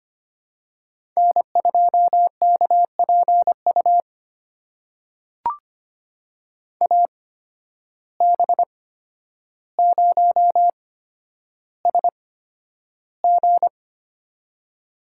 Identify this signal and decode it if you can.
Morse code, heard as a keyed tone, about 25 wpm, spaced as for 11 wpm, 700 Hz. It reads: N2KPU AB0SG
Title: USA Call Signs - 5 Characters - 8x Character Spacing - 25wpm